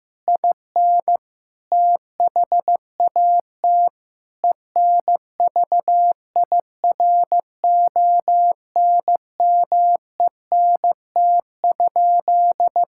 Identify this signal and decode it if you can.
Morse code, heard as a keyed tone, about 15 wpm, 700 Hz. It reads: IN THAT ENVIRONMENT?
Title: Sentences from Top 1000 Words - Review Part 5 - 15wpm